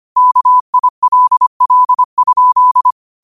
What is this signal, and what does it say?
Morse code, heard as a keyed tone, about 25 wpm, 1000 Hz. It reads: KILL?